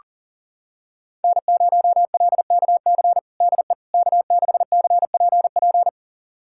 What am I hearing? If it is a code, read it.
Morse code, heard as a keyed tone, about 40 wpm, 700 Hz. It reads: N0LKC DE K6CPP
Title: USA Callsign DE Callsign - 5 Characters - 40wpm